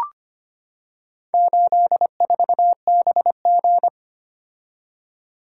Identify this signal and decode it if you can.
Morse code, heard as a keyed tone, about 25 wpm, 700 Hz. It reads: 84BG
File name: practice / 25wpm / Four Letter-Number Combinations - 25wpm